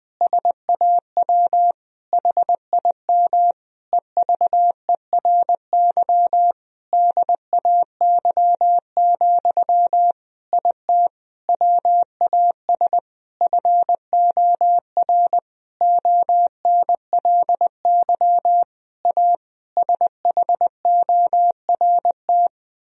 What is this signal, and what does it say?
Morse code, heard as a keyed tone, about 20 wpm, 700 Hz. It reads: SAW HIM EVERY DAY, IT WAS FOR ONLY A SHORT
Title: Sentences from Top 1000 Words - Review Part 3 - 20wpm